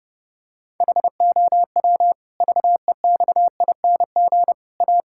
Morse code, heard as a keyed tone, about 30 wpm, 700 Hz. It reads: HOW VEXING A
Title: Pangram Challenge v2 - 30wpm